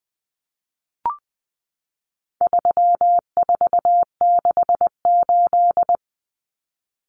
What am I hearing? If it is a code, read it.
Morse code, heard as a keyed tone, about 20 wpm, 700 Hz. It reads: 3468
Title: Numbers - 4-Digit 20wpm